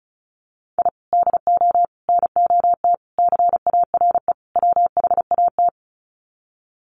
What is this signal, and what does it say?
Morse code, heard as a keyed tone, about 34 wpm, 700 Hz. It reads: I DO NOT CARE WHAT
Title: Sets of 5 Words 35wpm